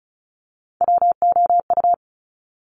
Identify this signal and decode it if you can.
Morse code, heard as a keyed tone, about 36 wpm, 700 Hz. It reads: WOU
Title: Top Three Letter Combinations in Words 35wpm